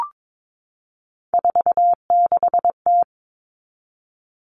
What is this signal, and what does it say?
Morse code, heard as a keyed tone, about 22 wpm, 700 Hz. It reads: 46T
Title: All Cut Numbers - 3-Digit 22wpm